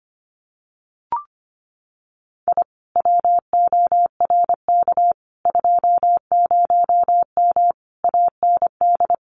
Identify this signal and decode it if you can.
Morse code, heard as a keyed tone, about 25 wpm, 700 Hz. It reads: I WORK 20M AND
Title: Common QSO Phrases Encore 25wpm